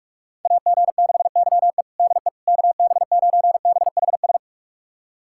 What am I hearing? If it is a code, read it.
Morse code, heard as a keyed tone, about 44 wpm, 690 Hz. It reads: AG6YE DE KB9BHS
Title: USA Callsign DE Callsign 45wpm